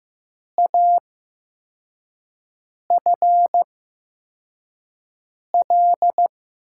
Morse code, heard as a keyed tone, about 15 wpm, spaced as for 4 wpm, 700 Hz. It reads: AFL
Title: USA Call Signs - 6 Characters - 8x Character Spacing - 15wpm